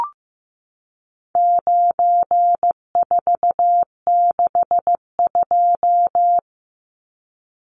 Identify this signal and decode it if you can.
Morse code, heard as a keyed tone, about 15 wpm, 700 Hz. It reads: 9462
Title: Numbers - 4-Digit 15wpm